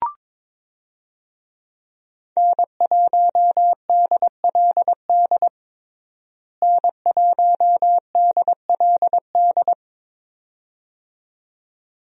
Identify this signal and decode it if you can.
Morse code, heard as a keyed tone, about 22 wpm, 700 Hz. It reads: N1DLD N1DLD
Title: USA Call Signs - 5 Characters - Repeated 2x - 22wpm